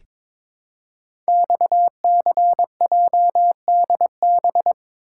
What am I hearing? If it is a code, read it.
Morse code, heard as a keyed tone, about 23 wpm, 700 Hz. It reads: XCJDB
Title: Code Groups - 5 Letters - 22wpm